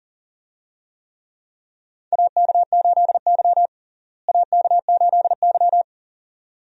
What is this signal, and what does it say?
Morse code, heard as a keyed tone, about 40 wpm, 700 Hz. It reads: AK8Y AK8Y
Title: USA Call Signs - 4 Characters - Repeated 2x - 40wpm